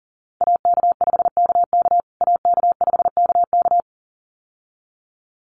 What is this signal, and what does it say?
Morse code, heard as a keyed tone, about 40 wpm, 700 Hz. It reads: AK5KK AK5KK
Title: USA Call Signs Repeated 2x 40wpm